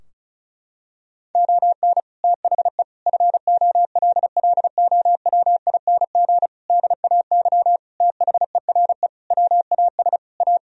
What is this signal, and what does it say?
Morse code, heard as a keyed tone, about 36 wpm, 700 Hz. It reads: ON THE FOLLOWING DAY THERE WAS A
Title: Sentences from Top 1000 Words - Review Part 4 - 35wpm